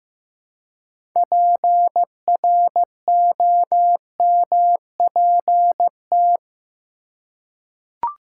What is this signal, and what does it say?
Morse code, heard as a keyed tone, about 15 wpm, 700 Hz. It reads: PROMPT
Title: Six Letter Words - 15wpm